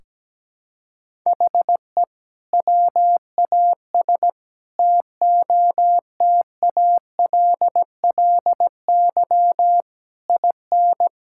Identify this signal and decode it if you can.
Morse code, heard as a keyed tone, about 17 wpm, 700 Hz. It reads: HE WAS TOTALLY IN
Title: Sentences from Top 1600 Words 17wpm